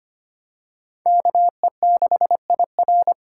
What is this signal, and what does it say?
Morse code, heard as a keyed tone, about 25 wpm, 700 Hz. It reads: KE6IR
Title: USA Call Signs - 5 Characters - 2x Character Spacing - 25wpm